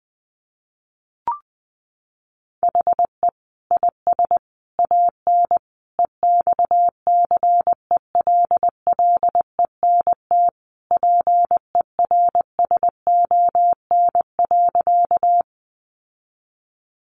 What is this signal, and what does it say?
Morse code, heard as a keyed tone, about 20 wpm, 700 Hz. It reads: HE IS AN EXCELLENT PERSON.
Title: Sentences from Top 1000 Words 20wpm